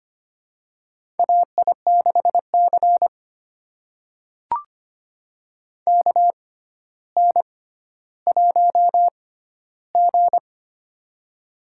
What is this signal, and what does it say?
Morse code, heard as a keyed tone, about 25 wpm, spaced as for 13 wpm, 700 Hz. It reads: AI6C KN1G
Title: USA Call Signs - 4 Characters - 6x Character Spacing - 25wpm